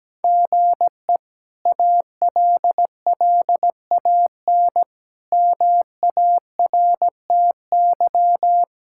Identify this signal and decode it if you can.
Morse code, heard as a keyed tone, about 17 wpm, 700 Hz. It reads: GE ALLAN MARTY
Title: K1USN SST QSO Practice - 17wpm